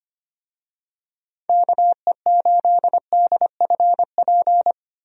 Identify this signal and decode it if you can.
Morse code, heard as a keyed tone, about 25 wpm, 700 Hz. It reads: KE8DFP
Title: USA Call Signs - 6 Characters - 25wpm